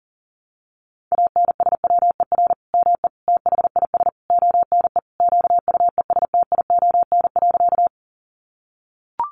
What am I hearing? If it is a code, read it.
Morse code, heard as a keyed tone, about 40 wpm, 700 Hz. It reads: ANSWER ME THIS ONE QUESTION.
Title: Sentences from Top 400 Words 40wpm